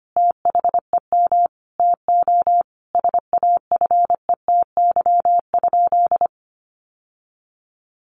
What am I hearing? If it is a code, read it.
Morse code, heard as a keyed tone, about 25 wpm, 700 Hz. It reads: THEM TO SAFETY?
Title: Sentences from Top 700 Words 25wpm